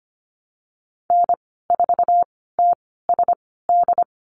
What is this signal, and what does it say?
Morse code, heard as a keyed tone, about 25 wpm, spaced as for 16 wpm, 700 Hz. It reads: N4TSD
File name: Lesson 15g - Number 5 - Call Signs 25wpm